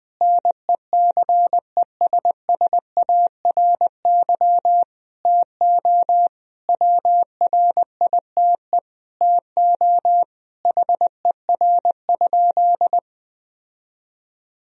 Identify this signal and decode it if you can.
Morse code, heard as a keyed tone, about 21 wpm, 700 Hz. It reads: NECESSARY TO WRITE TO HER?
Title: Sentences from Top 1000 Words - Review Part 1 - 20wpm